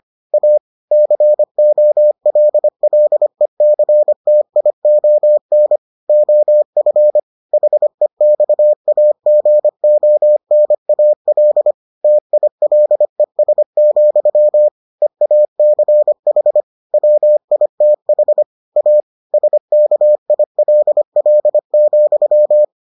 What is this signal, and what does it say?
Morse code, heard as a keyed tone, about 25 wpm, 600 Hz. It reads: A COLLECTION OF HEXAGONAL TILES, EACH WITH A SKILL,